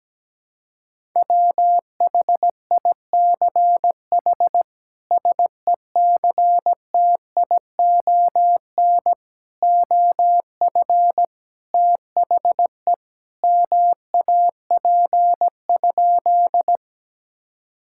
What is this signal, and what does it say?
Morse code, heard as a keyed tone, about 17 wpm, 700 Hz. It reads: WHICH SECTION OF THE MAP?
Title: Sentences from Top 1000 Words - Review Part 5 - 17wpm